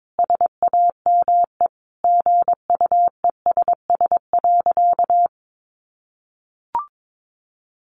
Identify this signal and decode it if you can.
Morse code, heard as a keyed tone, about 22 wpm, 700 Hz. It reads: SAME GUESS.
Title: Sentences from Top 1500 Words 22wpm